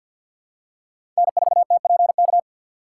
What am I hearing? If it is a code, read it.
Morse code, heard as a keyed tone, about 53 wpm, 700 Hz. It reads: N4TPK
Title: USA Call Signs - 4x Character Spacing - 50wpm